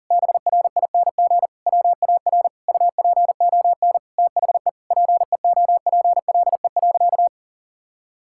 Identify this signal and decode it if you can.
Morse code, heard as a keyed tone, about 40 wpm, 700 Hz. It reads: BRING WAR UPON THE PEOPLE.
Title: Sentences from Top 1000 Words - Review Part 6 - 40wpm